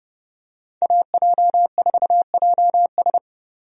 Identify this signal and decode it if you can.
Morse code, heard as a keyed tone, about 30 wpm, 700 Hz. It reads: AJ4JS
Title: USA Call Signs - 5 Characters - 30wpm